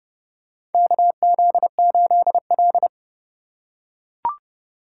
Morse code, heard as a keyed tone, about 30 wpm, 700 Hz. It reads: KZ8L
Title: USA Call Signs - 4 Characters - 4x Character Spacing - 30wpm